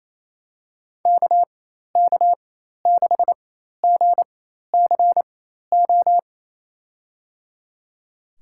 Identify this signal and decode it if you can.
Morse code, heard as a keyed tone, about 27 wpm, spaced as for 13 wpm, 700 Hz. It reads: KK6GCO
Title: USA Call Signs - 4x Character Spacing - 28wpm